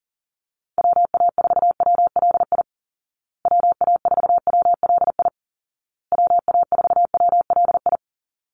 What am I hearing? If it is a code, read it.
Morse code, heard as a keyed tone, about 40 wpm, 700 Hz. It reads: WA4WLI WA4WLI WA4WLI
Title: Call Signs Repeated 2x - 40wpm